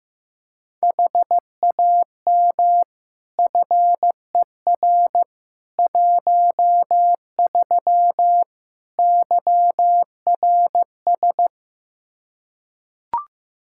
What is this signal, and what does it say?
Morse code, heard as a keyed tone, about 15 wpm, 700 Hz. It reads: HAM FER 13 YRS
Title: K1USN SST QSO Practice 15wpm